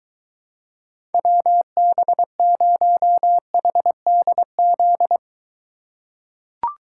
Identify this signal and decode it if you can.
Morse code, heard as a keyed tone, about 23 wpm, 700 Hz. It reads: WB0HDZ